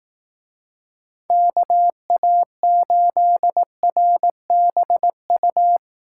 Morse code, heard as a keyed tone, about 18 wpm, 700 Hz. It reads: KA8RBU
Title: Speed Racing - USA 2x3 Call Signs 15wpm